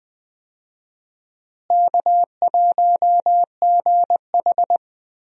Morse code, heard as a keyed tone, about 21 wpm, 700 Hz. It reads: K1GH
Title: USA Call Signs - 4 Characters - Repeated 3x - 20wpm